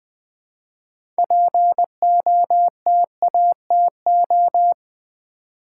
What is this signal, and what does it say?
Morse code, heard as a keyed tone, about 20 wpm, 700 Hz. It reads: POTATO
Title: Six Letter Words - 20wpm